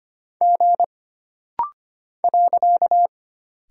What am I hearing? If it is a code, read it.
Morse code, heard as a keyed tone, about 25 wpm, 700 Hz. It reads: G .
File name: Single Character (Letter:Number:Punctuation:Prosign) - Rapid-Fire - 25wpm